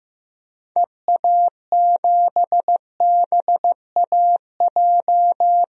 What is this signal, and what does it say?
Morse code, heard as a keyed tone, about 15 wpm, 700 Hz. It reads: EA7BAJ
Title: DXCC-100 Call Signs - 6 Characters - Repeated 3x - 15wpm